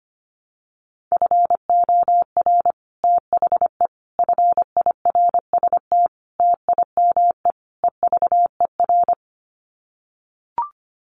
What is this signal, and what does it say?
Morse code, heard as a keyed tone, about 25 wpm, 700 Hz. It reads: FOR THE FIRST TIME EVER
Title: Sets of 5 Words Encore2 25wpm